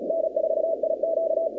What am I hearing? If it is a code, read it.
Morse code, heard as a keyed tone, about 36 wpm, 650 Hz, with noise. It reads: N4IQ
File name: contests / 2019-cw-wpx-kt5j-k5pi- -2330 0-47